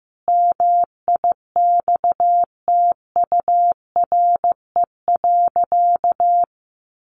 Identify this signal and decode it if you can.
Morse code, heard as a keyed tone, about 15 wpm, 700 Hz. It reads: MIXTURE.
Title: Sentences from Top 1500 Words 15wpm